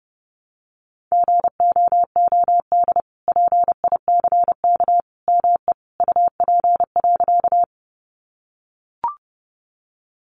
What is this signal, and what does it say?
Morse code, heard as a keyed tone, about 30 wpm, 700 Hz. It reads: GOOD PICK ME UP.
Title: Sentences from Top 1000 Words 30wpm